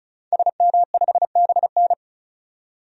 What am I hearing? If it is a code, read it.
Morse code, heard as a keyed tone, about 34 wpm, 700 Hz. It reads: SM5BN
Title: DXCC-100 Call Signs - 5 Characters - 35wpm